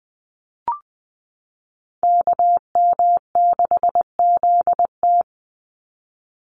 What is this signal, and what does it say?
Morse code, heard as a keyed tone, about 19 wpm, 700 Hz. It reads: KM6ZT
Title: Call Signs Repeated 2x - 20wpm